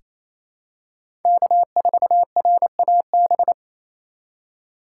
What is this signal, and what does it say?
Morse code, heard as a keyed tone, about 28 wpm, 700 Hz. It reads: K4RAB
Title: K1USN SST QSO Practice - 28wpm